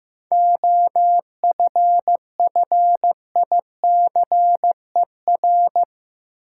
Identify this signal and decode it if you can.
Morse code, heard as a keyed tone, about 15 wpm, 700 Hz. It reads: OFFICER
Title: Seven Letter Words - 15wpm